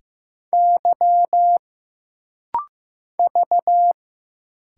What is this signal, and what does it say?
Morse code, heard as a keyed tone, about 15 wpm, 700 Hz. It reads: Y V
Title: Single Letters - Rapid-Fire 15wpm